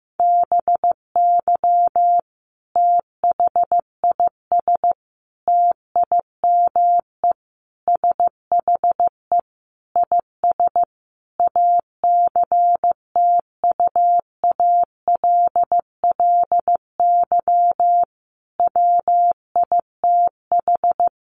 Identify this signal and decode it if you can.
Morse code, heard as a keyed tone, about 15 wpm, 700 Hz. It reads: BY THIS TIME SHE IS ACTUALLY WITH
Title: Sentences from Top 700 Words 15wpm